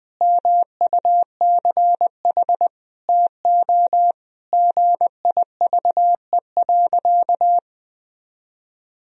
Morse code, heard as a keyed tone, about 20 wpm, 700 Hz. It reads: MUCH TO GIVE.
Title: Sentences from Top 1000 Words - Review Part 4 - 20wpm